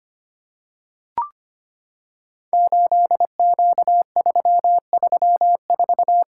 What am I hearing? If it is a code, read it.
Morse code, heard as a keyed tone, about 25 wpm, 700 Hz. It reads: 8Q334
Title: Code Groups - 5 Letter:Numbers - 25wpm